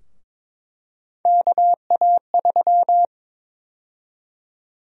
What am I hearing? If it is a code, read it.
Morse code, heard as a keyed tone, about 22 wpm, 700 Hz. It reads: KA3
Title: USA Call Sign Prefixes - Repeated 2x - 22wpm